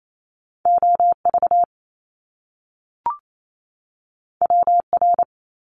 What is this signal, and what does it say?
Morse code, heard as a keyed tone, about 28 wpm, 700 Hz. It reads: OV WR
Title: Top Two Letter Combinations in Words 28wpm